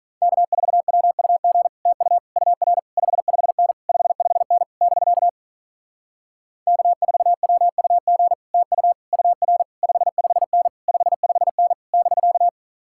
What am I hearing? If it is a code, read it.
Morse code, heard as a keyed tone, about 46 wpm, 700 Hz. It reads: K4WUG TU UR 55N 55N <BK> K4WUG TU UR 55N 55N <BK>